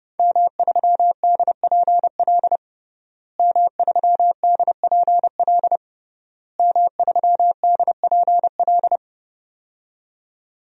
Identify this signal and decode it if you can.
Morse code, heard as a keyed tone, about 30 wpm, 700 Hz. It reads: M3DPL M3DPL M3DPL
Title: DXCC-100 Call Signs - 5 Characters - Repeated 3x - 30wpm